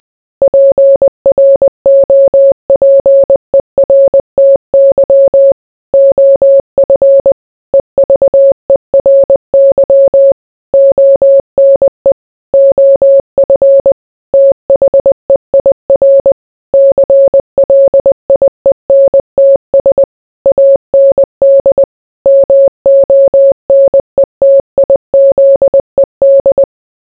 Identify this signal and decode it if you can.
Morse code, heard as a keyed tone, about 20 wpm, 550 Hz. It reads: PROPERTY OF EVERY ONE OF THEIR CLIENTS AND MONETIZED